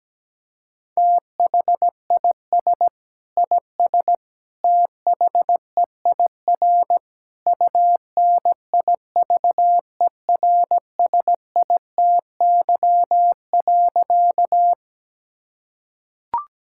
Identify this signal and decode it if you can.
Morse code, heard as a keyed tone, about 17 wpm, 700 Hz. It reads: THIS IS THEIR UNIVERSITY.
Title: Sentences from Top 1000 Words - Review Part 5 - 17wpm